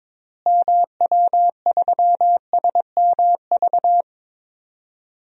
Morse code, heard as a keyed tone, about 22 wpm, 700 Hz. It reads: MW3SMV
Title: DXCC-100 Call Signs - 6 Characters - Repeated 2x - 22wpm